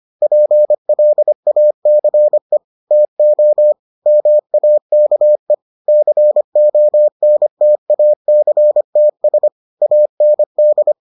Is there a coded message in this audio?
Yes: Morse code from 0.2 to 10.9 s, heard as a keyed tone, about 25 wpm, 600 Hz. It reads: PLACE TO MAKE CONTACTS AND